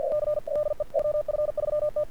Morse code, heard as a keyed tone, about 48 wpm, 1200 Hz, with radio noise. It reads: C DE WF3T
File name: archived / CRSnet